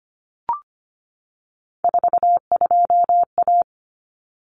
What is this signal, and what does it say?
Morse code, heard as a keyed tone, about 25 wpm, 700 Hz. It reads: 42A
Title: Common Cut Numbers - 3-Digit 25wpm